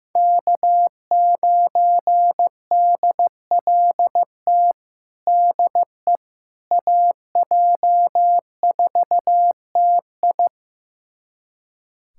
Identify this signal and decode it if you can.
Morse code, heard as a keyed tone, about 16 wpm, 700 Hz. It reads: K9DLT DE AJ4TI
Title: USA Callsign DE Callsign - 5 Characters - 15wpm